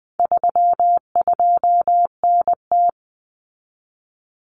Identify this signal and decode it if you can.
Morse code, heard as a keyed tone, about 20 wpm, 700 Hz. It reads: 32NT
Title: Common Cut Numbers - 4-Digit 20wpm